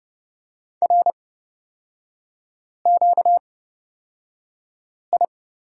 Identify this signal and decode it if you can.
Morse code, heard as a keyed tone, about 29 wpm, spaced as for 5 wpm, 700 Hz. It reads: RQI
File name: Lesson 38e - Letter q - Three Characters 30wpm